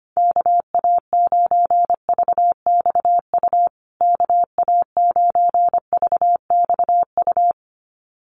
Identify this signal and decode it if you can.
Morse code, heard as a keyed tone, about 25 wpm, 700 Hz. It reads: KA9VXU KA9VXU
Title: USA Call Signs Repeated 2x 25wpm